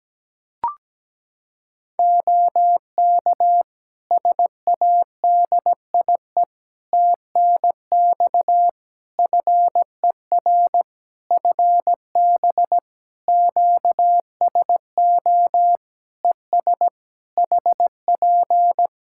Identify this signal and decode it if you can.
Morse code, heard as a keyed tone, about 17 wpm, 700 Hz. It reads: OK SADIE TNX FER FB QSO ES HP
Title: Practice QSO - Exchange 4 - Ending - 17wpm